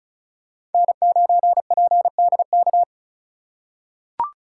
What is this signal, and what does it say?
Morse code, heard as a keyed tone, about 36 wpm, 700 Hz. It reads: N9PDK
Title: USA Call Signs - 5 Characters - 4x Character Spacing - 35wpm